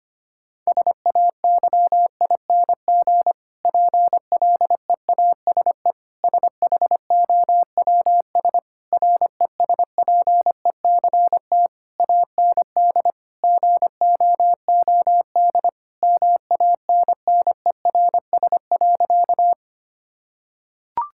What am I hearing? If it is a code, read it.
Morse code, heard as a keyed tone, about 25 wpm, 700 Hz. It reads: SAYING PLEASE SHOWS RESPECT AND GOOD MANNERS.
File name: practice / 25wpm / Difficult Sentences with Hints - 25wpm